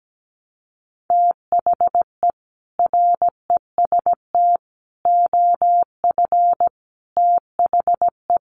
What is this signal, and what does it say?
Morse code, heard as a keyed tone, about 17 wpm, 700 Hz. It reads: THE REST OF THE
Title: Sets of 5 Words 17wpm